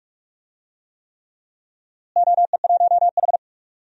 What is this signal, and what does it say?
Morse code, heard as a keyed tone, about 46 wpm, 700 Hz. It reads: OE1H
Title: DXCC-100 Call Signs - 4 Characters - Repeated 3x - 45wpm